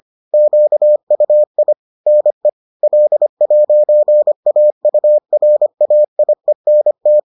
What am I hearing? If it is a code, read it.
Morse code, heard as a keyed tone, about 25 wpm, 600 Hz. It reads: QUI NE L'AURAIENT
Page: bulletin UFT sous forme texte et en CW